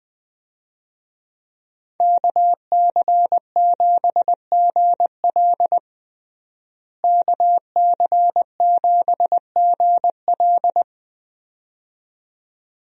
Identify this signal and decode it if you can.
Morse code, heard as a keyed tone, about 21 wpm, 700 Hz. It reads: KC7GL KC7GL
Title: USA Call Signs - 5 Characters - Repeated 2x - 20wpm